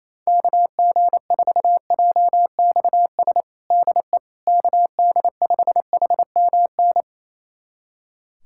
Morse code, heard as a keyed tone, about 28 wpm, 700 Hz. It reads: KG4JXS DE KD5HMN
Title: USA Callsign DE Callsign - 6 Characters - 28wpm